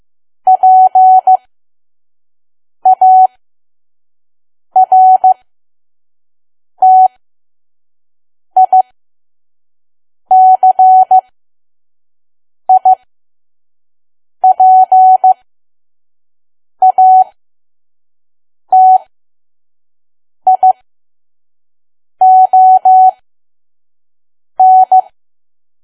Morse code, heard as a keyed tone, about 15 wpm, spaced as for 5 wpm, 750 Hz. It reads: PARTICIPATION
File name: Archive / 5 WPM files / PM